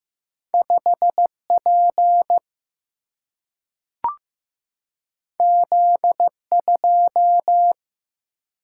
Morse code, heard as a keyed tone, about 15 wpm, 700 Hz. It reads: 5 P Z 2
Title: Single-Letter Single-Number 15wpm